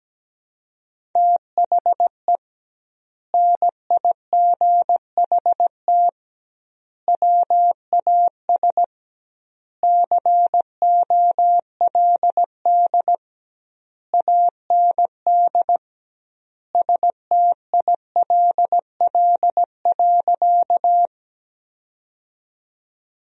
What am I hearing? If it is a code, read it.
Morse code, heard as a keyed tone, about 17 wpm, 700 Hz. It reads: THE NIGHT WAS COLD AND STILL.
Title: 2x Word-Spacing - Sentences from Top 400 Words - 17wpm